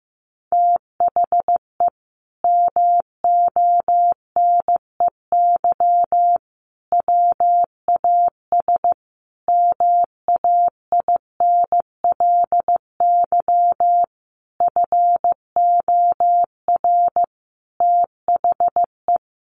Morse code, heard as a keyed tone, about 15 wpm, 700 Hz. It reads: THE MONEY WAS MAINLY FOR THE
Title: Sentences from Top 1400 Words 15wpm